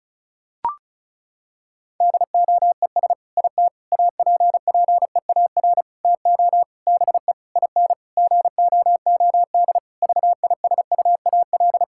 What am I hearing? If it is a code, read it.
Morse code, heard as a keyed tone, about 36 wpm, 700 Hz. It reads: DOES IT APPEAR TO BE IN GOOD VISUAL
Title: Sentences from Top 1000 Words - Review Part 6 - 35wpm